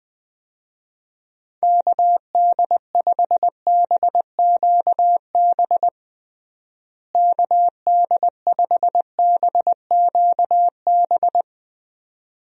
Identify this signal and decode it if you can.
Morse code, heard as a keyed tone, about 20 wpm, 700 Hz. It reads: KD5BQB KD5BQB
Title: USA Call Signs - 6 Characters - Repeated 2x - 20wpm